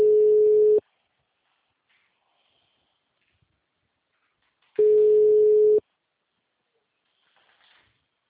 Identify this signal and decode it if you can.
Ringback tone (Europe, 425 Hz).
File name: out- -2015- - - .60494